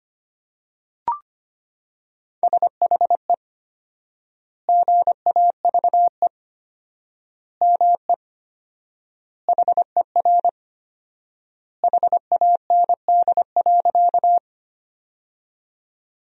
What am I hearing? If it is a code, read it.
Morse code, heard as a keyed tone, about 25 wpm, 700 Hz. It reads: SHE GAVE ME HER HAND.
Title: 4x Word-Spacing - Sentences from Top 500 Words - 25wpm